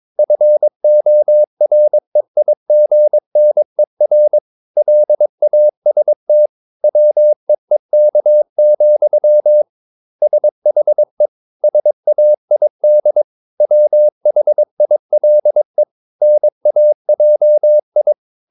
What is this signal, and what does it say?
Morse code, heard as a keyed tone, about 22 wpm, 600 Hz. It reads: FOREIGNER LAST WEEK, SHE SAID WHILE NAJI
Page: Romanen Never af Ken Follet på engelsk.